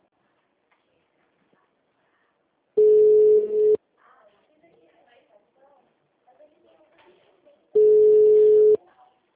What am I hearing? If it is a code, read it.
Ringback tone (Europe, 425 Hz), with speech.
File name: out- -2002- - - .75526